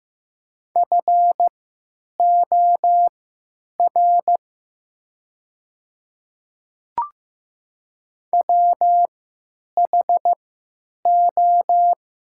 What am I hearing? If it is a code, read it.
Morse code, heard as a keyed tone, about 15 wpm, spaced as for 9 wpm, 700 Hz. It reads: FOR WHO
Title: 3x Character Spacing - Top 300 Words - 15wpm